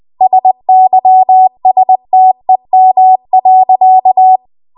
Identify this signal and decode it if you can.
Morse code, heard as a keyed tone, about 20 wpm, 750 Hz. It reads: SYSTEM.
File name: Archive / 20 WPM / 20WPM